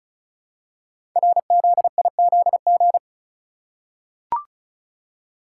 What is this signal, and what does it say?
Morse code, heard as a keyed tone, about 36 wpm, 700 Hz. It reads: RZIZG
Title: Code Groups - 5 Letters - 35wpm